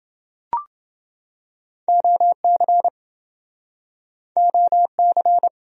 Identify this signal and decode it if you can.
Morse code, heard as a keyed tone, about 27 wpm, 700 Hz. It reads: OC OC